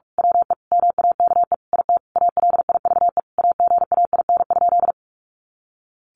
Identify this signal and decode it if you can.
Morse code, heard as a keyed tone, about 45 wpm, 700 Hz. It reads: WE MAKE IT ALIVE AGAIN?
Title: A Sound Of Thunder 45wpm